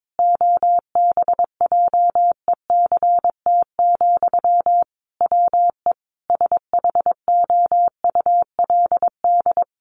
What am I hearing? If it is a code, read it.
Morse code, heard as a keyed tone, about 22 wpm, 700 Hz. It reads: OBJECT, WE SHOULD
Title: Sentences from Top 800 Words 22wpm